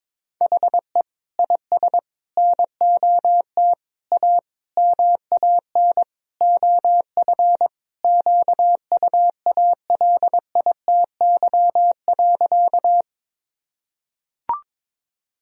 Morse code, heard as a keyed tone, about 21 wpm, 700 Hz. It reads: HE IS NOT A MAN OF QUALITY.
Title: Sentences from Top 1000 Words - Review Part 4 - 22wpm